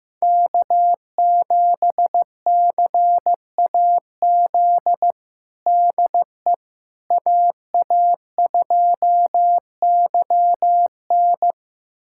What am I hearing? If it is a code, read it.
Morse code, heard as a keyed tone, about 16 wpm, 700 Hz. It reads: K7CAZ DE AA2YN
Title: USA Callsign DE Callsign - 5 Characters - 15wpm